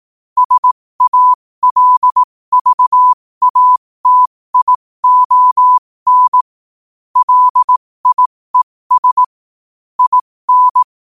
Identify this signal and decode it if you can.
Morse code, heard as a keyed tone, about 18 wpm, spaced as for 15 wpm, 1000 Hz. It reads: SALVATION LIES IN
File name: qotdfiles / 15wpm